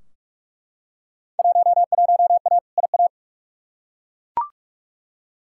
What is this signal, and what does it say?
Morse code, heard as a keyed tone, about 42 wpm, 700 Hz. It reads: 11A IA
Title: Field Day Exchanges Encore - 45wpm